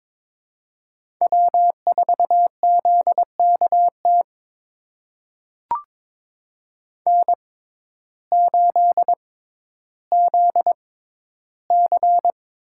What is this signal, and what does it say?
Morse code, heard as a keyed tone, about 22 wpm, spaced as for 12 wpm, 700 Hz. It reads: W4ZKT N8ZC
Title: USA Call Signs - 5 Characters - 6x Character Spacing - 22wpm